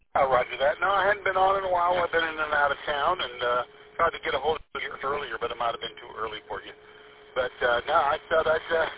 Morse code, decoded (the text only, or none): none